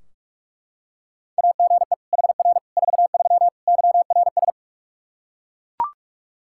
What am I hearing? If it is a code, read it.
Morse code, heard as a keyed tone, about 41 wpm, 700 Hz. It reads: AGE HR 43 YRS
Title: K1USN SST QSO Practice 45wpm